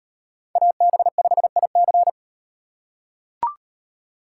Morse code, heard as a keyed tone, about 35 wpm, 700 Hz. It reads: AB5IC